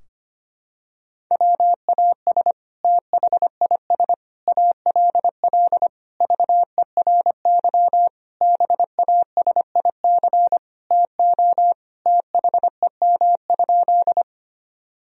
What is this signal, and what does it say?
Morse code, heard as a keyed tone, about 25 wpm, 700 Hz. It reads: WAS THIS ALL VERY BASIC TO THEM?
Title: Sentences from Top 1000 Words - Review Part 3 - 25wpm